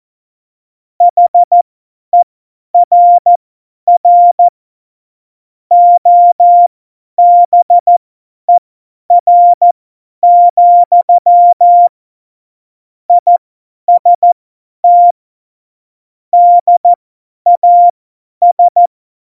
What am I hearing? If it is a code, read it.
Morse code, heard as a keyed tone, about 14 wpm, spaced as for 10 wpm, 700 Hz. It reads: HERR OBER, IST DAS
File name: Morsekurs / Witze / Witz 18. Juni 2019, 50 F70